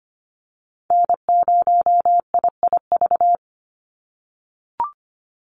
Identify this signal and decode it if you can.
Morse code, heard as a keyed tone, about 25 wpm, 700 Hz. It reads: N0IIV
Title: USA Call Signs 25wpm